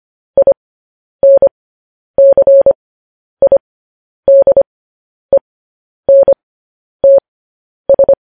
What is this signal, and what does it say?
Morse code, heard as a keyed tone, about 25 wpm, spaced as for 10 wpm, 550 Hz. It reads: INCIDENTS
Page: news headlines in morse code at 10 wpm